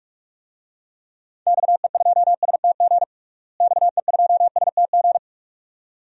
Morse code, heard as a keyed tone, about 41 wpm, 700 Hz. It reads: XE2STG XE2STG
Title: DXCC-100 Call Signs - 6 Characters - Repeated 2x - 45wpm